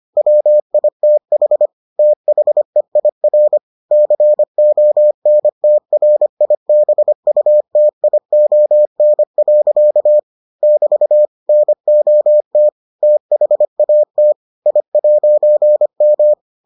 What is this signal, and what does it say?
Morse code, heard as a keyed tone, about 25 wpm, 600 Hz. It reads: WITH THEIR CONTRIBUTION. <BT> NOT THAT I'M